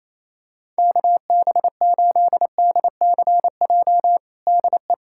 Morse code, heard as a keyed tone, about 27 wpm, 700 Hz. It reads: KB8DCJ DE
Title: USA Callsign DE Callsign - 6 Characters - 28wpm